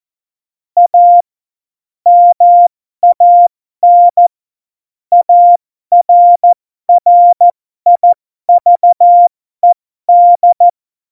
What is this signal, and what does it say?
Morse code, heard as a keyed tone, about 14 wpm, spaced as for 12 wpm, 700 Hz. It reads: A MAN ARRIVED
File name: Morsekurs / Witze